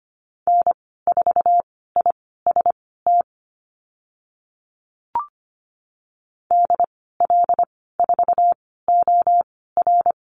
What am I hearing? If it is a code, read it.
Morse code, heard as a keyed tone, about 25 wpm, spaced as for 16 wpm, 700 Hz. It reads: N4IST DL4OR
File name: Lesson 15g - Number 5 - Call Signs 25wpm